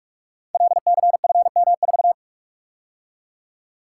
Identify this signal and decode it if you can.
Morse code, heard as a keyed tone, about 44 wpm, 700 Hz. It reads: LCFM4
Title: Code Groups - 5 Letter:Numbers - 45wpm